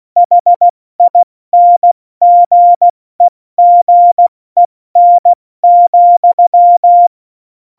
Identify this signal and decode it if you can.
Morse code, heard as a keyed tone, about 16 wpm, 700 Hz. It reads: HINGEGEN,
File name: Morsekurs / Witze / Witz 10.Dezember 2019, 70 F80